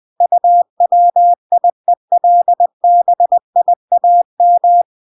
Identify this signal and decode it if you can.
Morse code, heard as a keyed tone, about 20 wpm, 700 Hz. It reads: UWIELBIAM